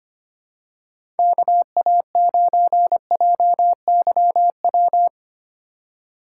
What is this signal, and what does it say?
Morse code, heard as a keyed tone, about 25 wpm, 700 Hz. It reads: KA9JYW
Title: USA Call Signs - 6 Characters - 2x Character Spacing - 25wpm